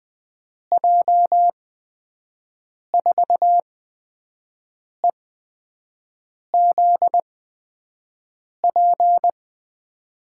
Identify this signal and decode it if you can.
Morse code, heard as a keyed tone, about 20 wpm, spaced as for 5 wpm, 700 Hz. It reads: J4EZP
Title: USA Call Signs - 6 Characters - 8x Character Spacing - 20wpm